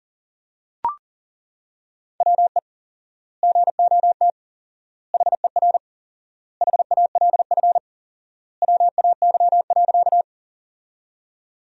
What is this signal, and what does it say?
Morse code, heard as a keyed tone, about 37 wpm, 700 Hz. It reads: WE GOT HER HALF WAY.
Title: 4x Word-Spacing - Sentences from Top 400 Words - 40wpm